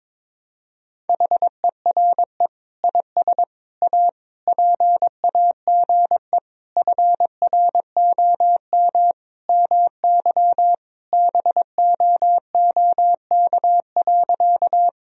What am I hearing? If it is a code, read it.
Morse code, heard as a keyed tone, about 22 wpm, 700 Hz. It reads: HERE IS A PAGE FROM MY BOOK.
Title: Sentences from Top 1000 Words - Review Part 1 - 22wpm